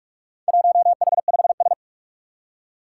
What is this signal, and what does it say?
Morse code, heard as a keyed tone, about 46 wpm, 700 Hz. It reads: 1H5S
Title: Four Challenging Letter-Number Combinations 45wpm